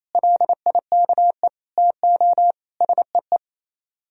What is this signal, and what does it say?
Morse code, heard as a keyed tone, about 35 wpm, 700 Hz. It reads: LIKE TO SEE